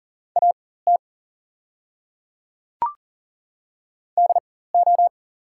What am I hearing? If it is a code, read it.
Morse code, heard as a keyed tone, about 37 wpm, spaced as for 18 wpm, 690 Hz. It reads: AT DO
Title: 4x Character Spacing - Top 100 Words - 40wpm